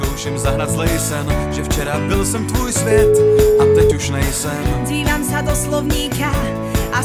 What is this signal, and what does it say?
Ringback tone (Europe, 425 Hz), with noise.